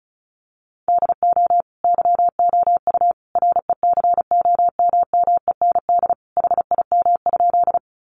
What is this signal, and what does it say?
Morse code, heard as a keyed tone, about 34 wpm, 700 Hz. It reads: DO YOU RECOMMEND HIM?
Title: Sentences from Top 1200 Words 35wpm